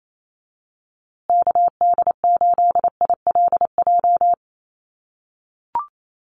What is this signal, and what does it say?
Morse code, heard as a keyed tone, about 28 wpm, 700 Hz. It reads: KD8ILJ
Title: USA Call Signs Repeated 3x 28wpm